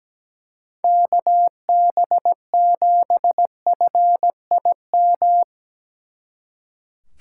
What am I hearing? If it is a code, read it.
Morse code, heard as a keyed tone, about 19 wpm, 700 Hz. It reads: KB7FIM
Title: Speed Racing - USA 2x3 Call Signs 17wpm